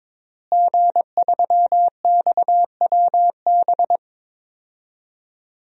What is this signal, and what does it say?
Morse code, heard as a keyed tone, about 22 wpm, 700 Hz. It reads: G3XWB
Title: DXCC-100 Call Signs - 5 Characters - Repeated 3x - 22wpm